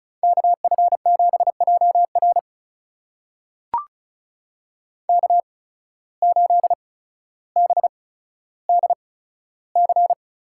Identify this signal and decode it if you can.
Morse code, heard as a keyed tone, about 36 wpm, spaced as for 15 wpm, 700 Hz. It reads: KF7JR K8BDC
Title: USA Call Signs - 5 Characters - 8x Character Spacing - 35wpm